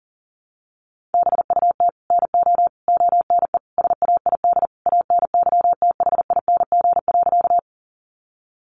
Morse code, heard as a keyed tone, about 40 wpm, 700 Hz. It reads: BUT NO ONE SAID ANYTHING.
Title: Sentences from Top 1000 Words 40wpm